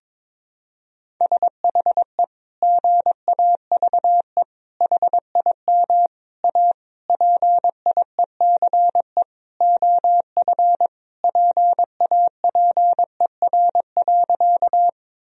Morse code, heard as a keyed tone, about 22 wpm, 700 Hz. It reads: SHE GAVE HIM A PIECE OF PAPER.
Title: Sentences from Top 1000 Words - Review Part 2 - 22wpm